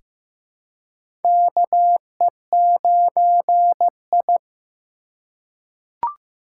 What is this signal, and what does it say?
Morse code, heard as a keyed tone, about 15 wpm, 700 Hz. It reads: KE9I
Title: K1USN SST QSO Practice - 15wpm